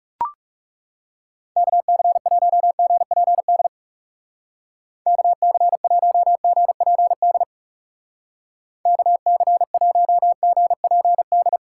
Morse code, heard as a keyed tone, about 36 wpm, 700 Hz. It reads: KC1GPD KC1GPD KC1GPD
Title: Speed Racing - USA 2x3 Call Signs 30wpm